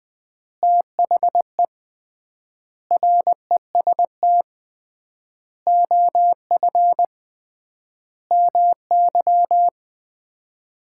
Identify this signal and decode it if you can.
Morse code, heard as a keyed tone, about 21 wpm, 700 Hz. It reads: THE REST OF MY
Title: 3x Word-Spacing - Sets of 5 Words - 20wpm